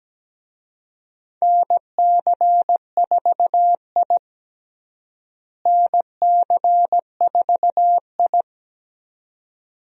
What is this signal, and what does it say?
Morse code, heard as a keyed tone, about 17 wpm, 700 Hz. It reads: NC4I NC4I
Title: USA Call Signs - 4 Characters - Repeated 2x - 17wpm